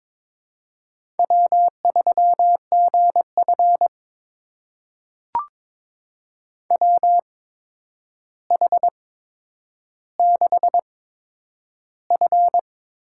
Morse code, heard as a keyed tone, about 22 wpm, spaced as for 9 wpm, 700 Hz. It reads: W3GF WH6F
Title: USA Call Signs - 8x Character Spacing - 22wpm